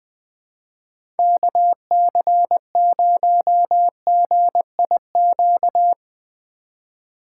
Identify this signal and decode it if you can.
Morse code, heard as a keyed tone, about 21 wpm, 700 Hz. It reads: KC0GIQ
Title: USA Call Signs - 6 Characters - 4x Character Spacing - 20wpm